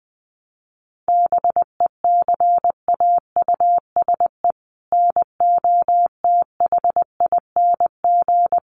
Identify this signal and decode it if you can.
Morse code, heard as a keyed tone, about 20 wpm, 700 Hz. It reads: BECAUSE NOTHING
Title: Sentences from Top 1100 Words 20wpm